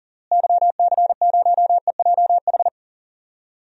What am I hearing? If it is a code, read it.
Morse code, heard as a keyed tone, about 40 wpm, 700 Hz. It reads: YC0EJH
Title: DXCC-100 Call Signs - 6 Characters - Repeated 3x - 40wpm